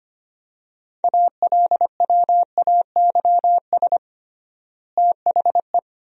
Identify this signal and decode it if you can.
Morse code, heard as a keyed tone, about 25 wpm, 700 Hz. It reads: ALWAYS THE
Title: 3x Word-Spacing - Sentences from Top 300 Words - 25wpm